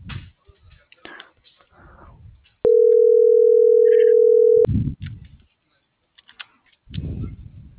Ringback tone (North America), 440 + 480 Hz.